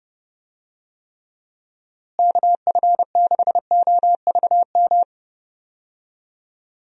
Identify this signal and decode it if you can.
Morse code, heard as a keyed tone, about 30 wpm, 700 Hz. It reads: KF6OVM
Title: USA Call Signs - 6 Characters - Repeated 2x - 30wpm